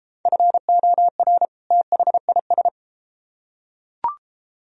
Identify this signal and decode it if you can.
Morse code, heard as a keyed tone, about 33 wpm, 700 Hz. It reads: FOR THIS